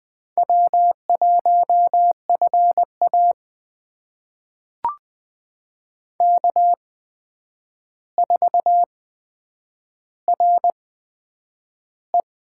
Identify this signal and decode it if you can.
Morse code, heard as a keyed tone, about 20 wpm, spaced as for 9 wpm, 700 Hz. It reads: W1FA K4RE
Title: USA Call Signs - 4 Characters - 8x Character Spacing - 20wpm